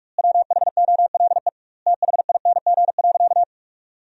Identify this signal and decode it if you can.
Morse code, heard as a keyed tone, about 46 wpm, 700 Hz. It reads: WHOLE THING.
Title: 2x Word-Spacing - Sentences from Top 400 Words - 45wpm